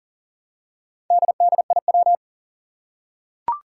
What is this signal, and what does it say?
Morse code, heard as a keyed tone, about 40 wpm, 700 Hz. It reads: DDIW
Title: Four Challenging Letter-Number Combinations 40wpm